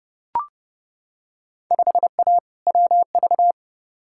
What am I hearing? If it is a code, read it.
Morse code, heard as a keyed tone, about 29 wpm, 700 Hz. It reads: 5A WV
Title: Field Day Exchanges - 30wpm